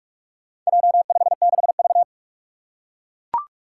Morse code, heard as a keyed tone, about 43 wpm, 700 Hz. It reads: J56V
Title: Four Challenging Letter-Number Combinations 45wpm